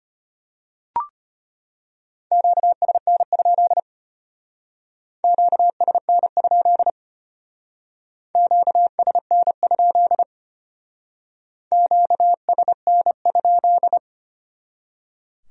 Morse code, heard as a keyed tone, about 32 wpm, 700 Hz. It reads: QSN? QSN? QSN? QSN?